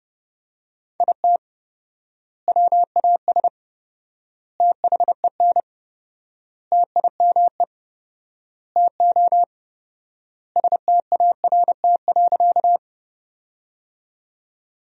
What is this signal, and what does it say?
Morse code, heard as a keyed tone, about 30 wpm, 700 Hz. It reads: IT WAS THEN TIME TO START.
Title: 4x Word-Spacing - Sentences from Top 300 Words - 30wpm